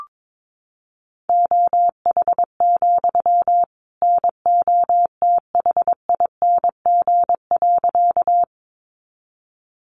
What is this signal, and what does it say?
Morse code, heard as a keyed tone, about 22 wpm, 700 Hz. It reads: OH, NOTHING.
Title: Sentences from Top 500 Words 22wpm